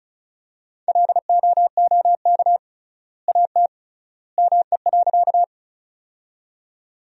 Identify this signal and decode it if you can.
Morse code, heard as a keyed tone, about 36 wpm, 700 Hz. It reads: LOOK AT ME.
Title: 3x Word-Spacing - Sentences from Top 200 Words - 35wpm